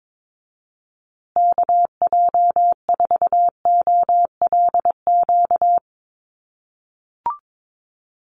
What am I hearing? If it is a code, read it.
Morse code, heard as a keyed tone, about 22 wpm, 700 Hz. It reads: KJ4OLQ
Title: USA Call Signs Repeated 2x 22wpm